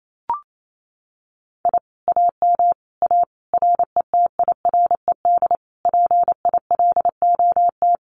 Morse code, heard as a keyed tone, about 28 wpm, 700 Hz. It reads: I AM A RETIRED PILOT
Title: Common QSO Phrases Encore 28wpm